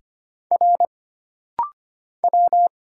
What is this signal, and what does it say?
Morse code, heard as a keyed tone, about 25 wpm, 700 Hz. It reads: R W
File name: Single Character (Letter:Number:Punctuation:Prosign) - Rapid-Fire - 25wpm